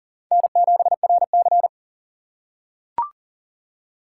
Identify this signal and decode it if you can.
Morse code, heard as a keyed tone, about 40 wpm, 700 Hz. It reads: N7RC
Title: USA Call Signs - 4 Characters - 4x Character Spacing - 40wpm